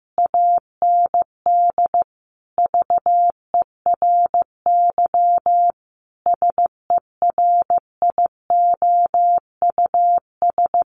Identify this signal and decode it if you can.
Morse code, heard as a keyed tone, about 15 wpm, 700 Hz. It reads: AND VERY SERIOUS
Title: Sentences from Top 1400 Words 15wpm